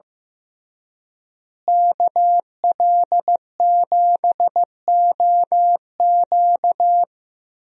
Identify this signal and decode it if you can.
Morse code, heard as a keyed tone, about 15 wpm, 700 Hz. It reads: KL7OQ
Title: DXCC-100 Call Signs - 5 Characters - Repeated 3x - 15wpm